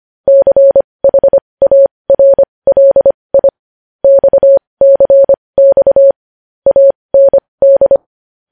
Morse code, heard as a keyed tone, about 25 wpm, spaced as for 20 wpm, 550 Hz. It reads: CHARLI XCX AND
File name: podcast / News headlines-1x-25x20